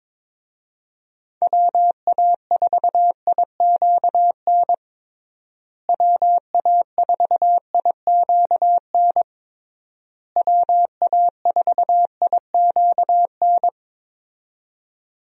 Morse code, heard as a keyed tone, about 22 wpm, 700 Hz. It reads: WA4IQN WA4IQN WA4IQN
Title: USA Call Signs - 6 Characters - Repeated 3x - 22wpm